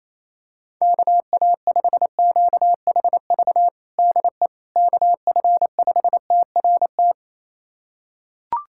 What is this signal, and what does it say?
Morse code, heard as a keyed tone, about 28 wpm, 700 Hz. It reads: KA5QHV DE KF5TRT
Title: USA Callsign DE Callsign - 6 Characters - 28wpm